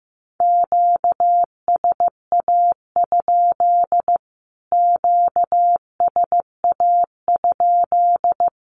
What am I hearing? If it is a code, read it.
Morse code, heard as a keyed tone, about 15 wpm, 700 Hz. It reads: QSA? QSA?
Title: Q-Codes Repeated 3x 15wpm